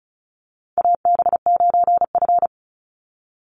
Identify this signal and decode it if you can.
Morse code, heard as a keyed tone, about 34 wpm, 700 Hz. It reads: AB9F
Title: USA Call Signs 35wpm